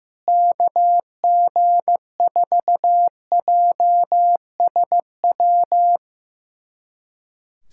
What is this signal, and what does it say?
Morse code, heard as a keyed tone, about 15 wpm, 700 Hz. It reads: KG4JSW
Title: SOTA QSO Practice 15wpm